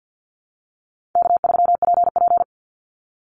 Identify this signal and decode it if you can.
Morse code, heard as a keyed tone, about 50 wpm, 700 Hz. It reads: K3PP
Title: Easy Call Signs - 50wpm